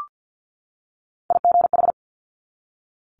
Morse code, heard as a keyed tone, about 50 wpm, 700 Hz. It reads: IGH
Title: Multi-letter Phonemes 50wpm